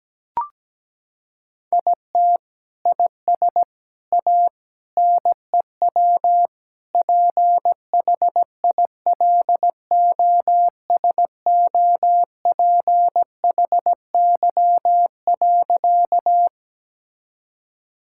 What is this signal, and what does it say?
Morse code, heard as a keyed tone, about 17 wpm, 700 Hz. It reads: IT IS A NEW PHILOSOPHY.
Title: Sentences from Top 1000 Words - Review Part 6 - 17wpm